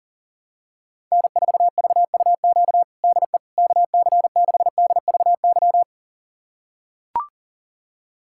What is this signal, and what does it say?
Morse code, heard as a keyed tone, about 40 wpm, 700 Hz. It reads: N4VUQ DE KC6DVY
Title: USA Callsign DE Callsign 40wpm